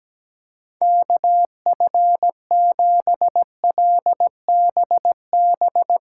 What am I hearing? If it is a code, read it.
Morse code, heard as a keyed tone, about 17 wpm, 700 Hz. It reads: KF7LBB
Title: Speed Racing - USA 2x3 Call Signs 17wpm